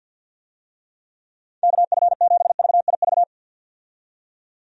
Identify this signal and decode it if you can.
Morse code, heard as a keyed tone, about 53 wpm, 700 Hz. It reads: KF7VIV
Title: USA Call Signs - 6 Characters - 50wpm